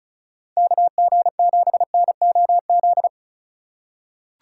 Morse code, heard as a keyed tone, about 36 wpm, 700 Hz. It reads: KG7NOZ
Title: USA Call Signs - 6 Characters - 4x Character Spacing - 35wpm